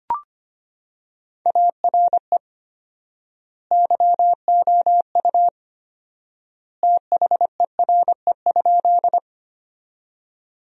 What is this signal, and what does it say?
Morse code, heard as a keyed tone, about 25 wpm, 700 Hz. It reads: ARE YOU THERE?
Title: 4x Word-Spacing - Sentences from Top 100 Words - 25wpm